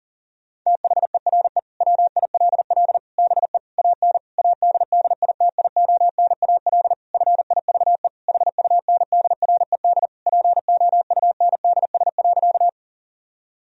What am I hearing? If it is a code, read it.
Morse code, heard as a keyed tone, about 40 wpm, 700 Hz. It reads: THERE WILL BE AN ADDITIONAL FIVE HUNDRED POUNDS.
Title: Sentences from Top 1000 Words - Review Part 4 - 40wpm